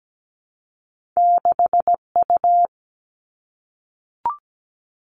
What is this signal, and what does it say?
Morse code, heard as a keyed tone, about 17 wpm, 700 Hz. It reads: 6U
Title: All Cut Numbers - 2-Digit 17wpm